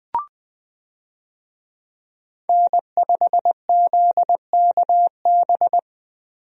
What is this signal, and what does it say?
Morse code, heard as a keyed tone, about 20 wpm, 700 Hz. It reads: N5ZKB
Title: USA Call Signs - 5 Characters - 20wpm